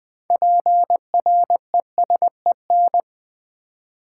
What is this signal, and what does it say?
Morse code, heard as a keyed tone, about 20 wpm, 700 Hz. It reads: PRESEN
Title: Build A Word - 3 Letters At A Time - 8-Letter Words - 20wpm